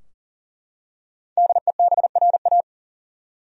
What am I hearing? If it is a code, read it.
Morse code, heard as a keyed tone, about 37 wpm, 700 Hz. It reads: DEBRA
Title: USA Common Names - 40wpm